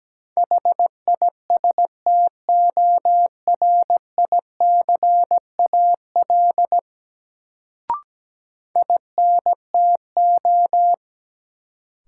Morse code, heard as a keyed tone, about 17 wpm, 700 Hz. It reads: HISTORICAL INTO
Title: Sentences from Top 600 Words - New Word Drill - 17wpm